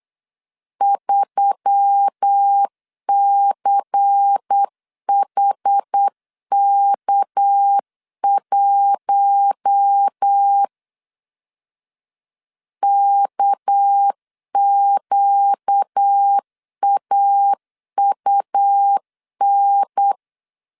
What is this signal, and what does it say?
Morse code, heard as a keyed tone, about 8 wpm, 800 Hz. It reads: 3CHK1 KQAUN
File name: LEZIONI / 6 PKX0 / 6 PK 8wpm 3003